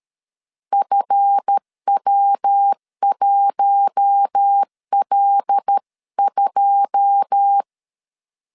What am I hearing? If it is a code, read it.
Morse code, heard as a keyed tone, about 13 wpm, 800 Hz. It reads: FW1L2
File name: LEZIONI / 10 SIGNS / 10D COMA 12wpm 2544